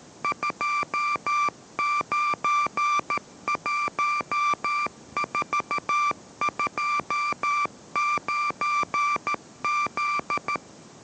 Morse code, heard as a keyed tone, about 13 wpm, 1180 Hz. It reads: 291429Z